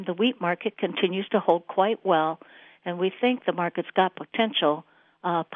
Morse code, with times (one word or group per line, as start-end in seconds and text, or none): none